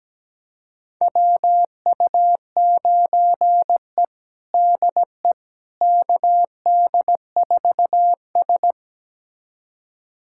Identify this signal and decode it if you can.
Morse code, heard as a keyed tone, about 17 wpm, 700 Hz. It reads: WU9E DE KD4S
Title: USA Callsign DE Callsign - 4 Characters - 17wpm